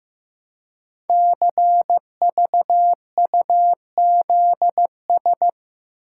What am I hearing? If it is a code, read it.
Morse code, heard as a keyed tone, about 15 wpm, 700 Hz. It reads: CVUZS
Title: Code Groups - 5 Letters - 15wpm